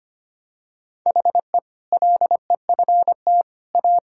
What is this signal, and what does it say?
Morse code, heard as a keyed tone, about 25 wpm, 700 Hz. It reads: HE LEFT A
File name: practice / 25wpm / Sentences with 4-Word Phrases - 25wpm